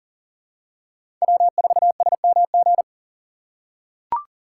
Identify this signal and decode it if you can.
Morse code, heard as a keyed tone, about 40 wpm, 700 Hz. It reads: W4SMG
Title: USA Call Signs - 5 Characters - 3x Character Spacing - 40wpm